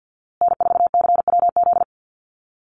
Morse code, heard as a keyed tone, about 50 wpm, 700 Hz. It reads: N4CWZ
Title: USA Call Signs Repeated 3x 50wpm